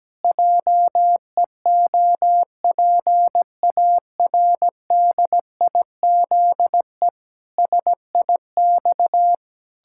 Morse code, heard as a keyed tone, about 17 wpm, 700 Hz. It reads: JEOPARDIZE SIX
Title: Pangram Challenge v2 - 17wpm